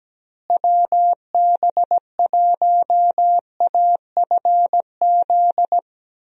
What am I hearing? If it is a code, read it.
Morse code, heard as a keyed tone, about 17 wpm, 700 Hz. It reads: WB1AFZ
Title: USA Call Signs - 6 Characters - Repeated 3x - 17wpm